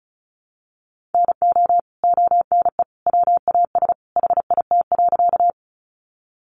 Morse code, heard as a keyed tone, about 34 wpm, 700 Hz. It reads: NO ONE WAS HIT.
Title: Sentences from Top 1200 Words 35wpm